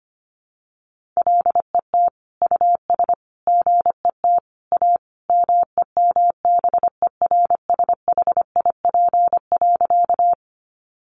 Morse code, heard as a keyed tone, about 25 wpm, 700 Hz. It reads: LET US GET A MEMBERSHIP.
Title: Sentences from Top 1400 Words 25wpm